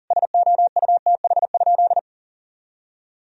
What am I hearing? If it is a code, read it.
Morse code, heard as a keyed tone, about 40 wpm, 700 Hz. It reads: SOUTH?
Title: Sentences from Top 1000 Words - Review Part 2 - 40wpm